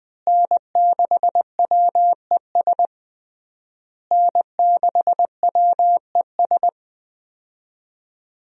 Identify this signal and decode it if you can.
Morse code, heard as a keyed tone, about 21 wpm, 700 Hz. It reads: N6WES N6WES
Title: USA Call Signs - 5 Characters - Repeated 3x - 20wpm